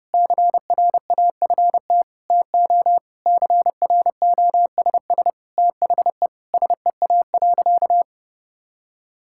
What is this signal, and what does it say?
Morse code, heard as a keyed tone, about 32 wpm, 700 Hz. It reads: CRAFT TO CROSS THE SEA.
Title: Sentences from Top 1000 Words - Review Part 6 - 30wpm